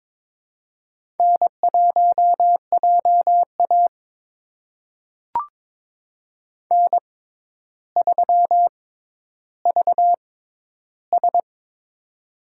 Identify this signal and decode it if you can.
Morse code, heard as a keyed tone, about 22 wpm, spaced as for 11 wpm, 700 Hz. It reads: N1JA N3VS
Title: USA Call Signs - 4 Characters - 6x Character Spacing - 22wpm